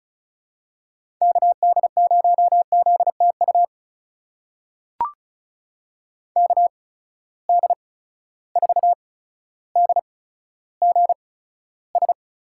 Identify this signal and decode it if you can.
Morse code, heard as a keyed tone, about 34 wpm, spaced as for 15 wpm, 700 Hz. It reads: KD0ZTU KD4DGS
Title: USA Call Signs - 6 Characters - 8x Character Spacing - 35wpm